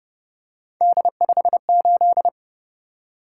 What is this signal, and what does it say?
Morse code, heard as a keyed tone, about 30 wpm, 700 Hz. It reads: D58